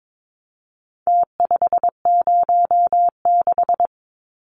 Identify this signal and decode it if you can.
Morse code, heard as a keyed tone, about 22 wpm, 700 Hz. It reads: T506
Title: Numbers - Contest Style - 4-Digit 22wpm